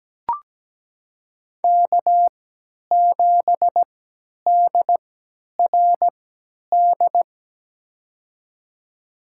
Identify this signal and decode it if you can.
Morse code, heard as a keyed tone, about 17 wpm, spaced as for 10 wpm, 700 Hz. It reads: K7DRD
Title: USA Call Signs - 5 Characters - 3x Character Spacing - 17wpm